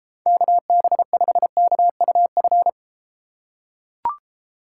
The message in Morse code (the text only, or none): KB5KUF